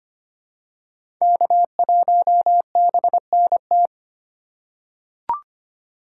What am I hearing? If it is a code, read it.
Morse code, heard as a keyed tone, about 25 wpm, 700 Hz. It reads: K1BNT
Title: USA Call Signs - 5 Characters - 3x Character Spacing - 25wpm